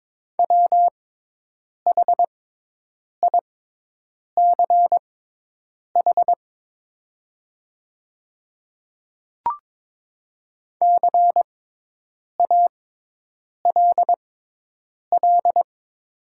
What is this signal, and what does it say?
Morse code, heard as a keyed tone, about 22 wpm, spaced as for 8 wpm, 700 Hz. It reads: WHICH CALL
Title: 6x Character Spacing - Top 100 Words - 22wpm